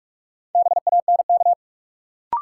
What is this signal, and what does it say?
Morse code, heard as a keyed tone, about 43 wpm, 700 Hz. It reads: BANK
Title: Sentences from Top 900 Words - New Word Drill - 45wpm